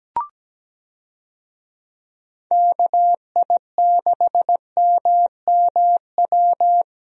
Morse code, heard as a keyed tone, about 17 wpm, 700 Hz. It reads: KI6MMW
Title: USA Call Signs - 6 Characters - Repeated 2x - 17wpm